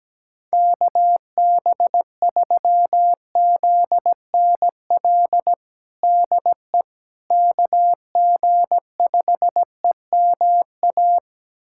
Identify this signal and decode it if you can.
Morse code, heard as a keyed tone, about 17 wpm, 700 Hz. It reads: KB3ZNL DE KG5EMA
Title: USA Callsign DE Callsign - 6 Characters - 17wpm